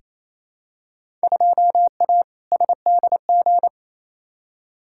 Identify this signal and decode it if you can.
Morse code, heard as a keyed tone, about 28 wpm, 700 Hz. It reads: 2A SDG
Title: Field Day Exchanges Encore - 28wpm